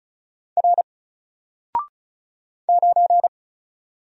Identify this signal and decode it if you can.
Morse code, heard as a keyed tone, about 34 wpm, 700 Hz. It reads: R 9
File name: Single-Letter-Number-35wpm